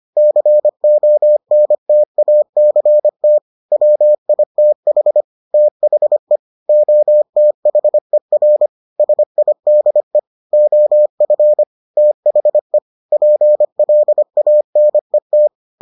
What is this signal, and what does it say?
Morse code, heard as a keyed tone, about 25 wpm, 600 Hz. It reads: CONTACT WITH THE OTHER SIDE OF THE PLANET